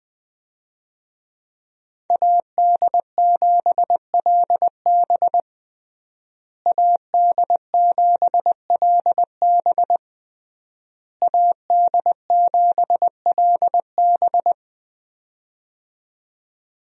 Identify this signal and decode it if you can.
Morse code, heard as a keyed tone, about 20 wpm, 700 Hz. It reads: AD7LB AD7LB AD7LB
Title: USA Call Signs - 5 Characters - Repeated 3x - 20wpm